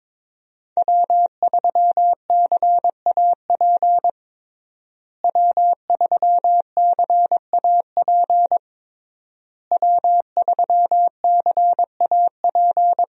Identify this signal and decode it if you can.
Morse code, heard as a keyed tone, about 22 wpm, 700 Hz. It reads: W3CAP W3CAP W3CAP
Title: USA Call Signs - 5 Characters - Repeated 3x - 22wpm